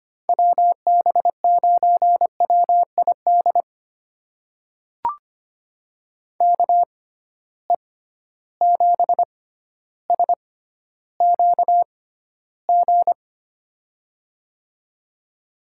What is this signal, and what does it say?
Morse code, heard as a keyed tone, about 25 wpm, spaced as for 13 wpm, 700 Hz. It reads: WB9WID KE7SQG
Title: USA Call Signs - 6 Characters - 6x Character Spacing - 25wpm